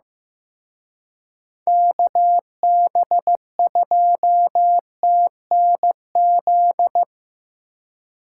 Morse code, heard as a keyed tone, about 18 wpm, 700 Hz. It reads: KB2TNZ
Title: Speed Racing - USA 2x3 Call Signs 15wpm